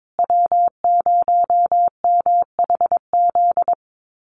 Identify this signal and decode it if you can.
Morse code, heard as a keyed tone, about 22 wpm, 700 Hz. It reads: W0MHZ
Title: USA Call Signs Repeated 3x 22wpm